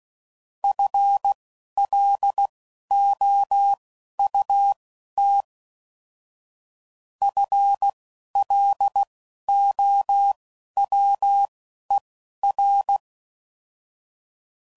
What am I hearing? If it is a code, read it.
Morse code, heard as a keyed tone, about 16 wpm, spaced as for 11 wpm, 800 Hz. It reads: FLOUT FLOWER